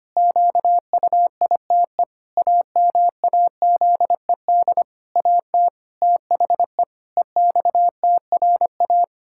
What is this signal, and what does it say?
Morse code, heard as a keyed tone, about 25 wpm, 700 Hz. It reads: QUITE AMAZED AT THE EXTRA
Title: Pangram Challenge v2 - 25wpm